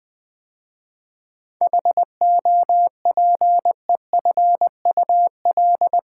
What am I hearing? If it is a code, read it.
Morse code, heard as a keyed tone, about 20 wpm, 700 Hz. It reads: HOPEFUL
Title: 7 Letter Words - First 3 Letters Given - 20wpm